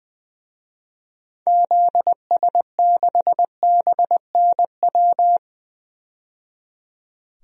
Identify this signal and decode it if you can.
Morse code, heard as a keyed tone, about 20 wpm, 700 Hz. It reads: ZS6BNW
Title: DXCC-100 Call Signs - 6 Characters - 20wpm